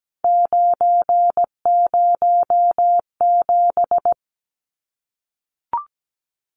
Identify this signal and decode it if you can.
Morse code, heard as a keyed tone, about 17 wpm, 700 Hz. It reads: 907
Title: Numbers - 3-Digit 17wpm